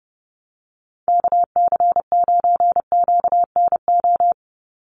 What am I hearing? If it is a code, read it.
Morse code, heard as a keyed tone, about 30 wpm, 700 Hz. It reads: KC9QNO
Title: USA Call Signs 30wpm